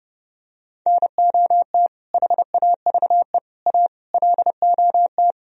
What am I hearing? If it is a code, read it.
Morse code, heard as a keyed tone, about 32 wpm, 700 Hz. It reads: NOT HAVE A LOT